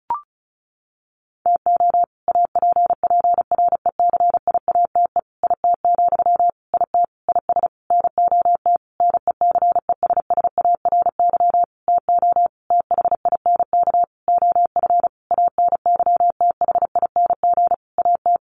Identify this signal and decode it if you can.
Morse code, heard as a keyed tone, about 34 wpm, 700 Hz. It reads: TO APPRECIATE IT, IT IS NOT NECESSARY TO THINK OF ANYTHING AT
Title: Sentences from Top 1300 Words 35wpm